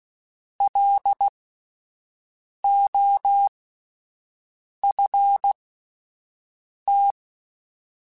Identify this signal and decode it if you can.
Morse code, heard as a keyed tone, about 16 wpm, spaced as for 5 wpm, 800 Hz. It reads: LOFT
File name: Alphabetical / Alphabetical mp3s / Ls @5